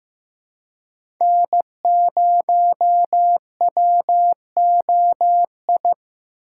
Morse code, heard as a keyed tone, about 15 wpm, 700 Hz. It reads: N0WOI
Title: K1USN SST QSO Practice - 15wpm